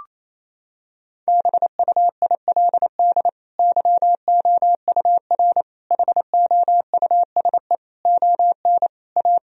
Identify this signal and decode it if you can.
Morse code, heard as a keyed tone, about 28 wpm, 700 Hz. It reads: BUILD YOUR HOUSE ON A
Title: Sentences from Top 1000 Words - Review Part 2 - 28wpm